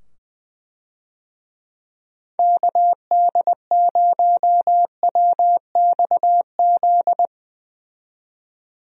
Morse code, heard as a keyed tone, about 20 wpm, 700 Hz. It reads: KD0WXZ
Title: USA Call Signs - 6 Characters - 20wpm